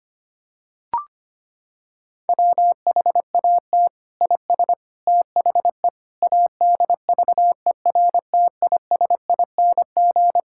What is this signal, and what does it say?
Morse code, heard as a keyed tone, about 25 wpm, 700 Hz. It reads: WHAT IS THE ADVERTISING
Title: Sentences from Top 1000 Words - Review Part 6 - 25wpm